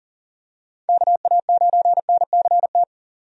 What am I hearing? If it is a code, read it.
Morse code, heard as a keyed tone, about 40 wpm, 700 Hz. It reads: KA9NCT
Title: USA Call Signs - 6 Characters - 40wpm